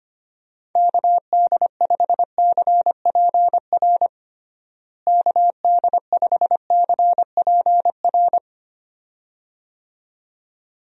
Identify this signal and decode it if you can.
Morse code, heard as a keyed tone, about 25 wpm, 700 Hz. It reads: KD5CPR KD5CPR
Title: USA Call Signs - 6 Characters - Repeated 2x - 25wpm